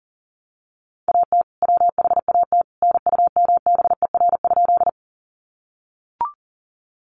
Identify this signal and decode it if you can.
Morse code, heard as a keyed tone, about 40 wpm, 700 Hz. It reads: AT WHAT NUMBER?
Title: Sentences from Top 100 Words 40wpm